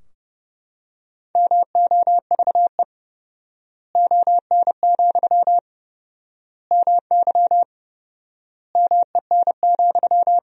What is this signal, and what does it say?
Morse code, heard as a keyed tone, about 32 wpm, 700 Hz. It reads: MOVE ON, MY MEN,
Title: 4x Word-Spacing - Sentences from Top 500 Words - 30wpm